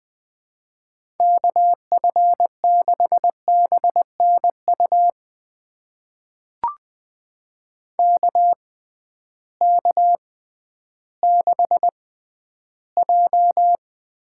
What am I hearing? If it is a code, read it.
Morse code, heard as a keyed tone, about 20 wpm, spaced as for 12 wpm, 700 Hz. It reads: KF6BNU KK6J
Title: USA Call Signs - 6 Characters - 6x Character Spacing - 20wpm